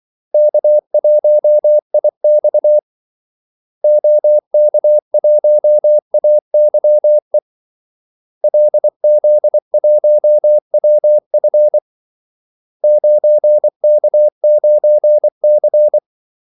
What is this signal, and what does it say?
Morse code, heard as a keyed tone, about 24 wpm, 600 Hz. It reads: K1IX OK1AYE LZ1WF 9K9C
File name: Callsigns-24wpm